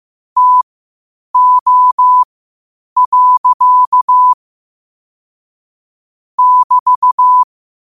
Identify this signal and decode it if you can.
Morse code, heard as a keyed tone, about 15 wpm, spaced as for 8 wpm, 1000 Hz. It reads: TO. <BT>
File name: qotdfiles / 8wpm